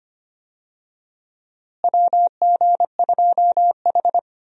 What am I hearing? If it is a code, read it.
Morse code, heard as a keyed tone, about 25 wpm, 700 Hz. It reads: WG2H
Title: USA Call Signs - 4 Characters - Repeated 2x - 25wpm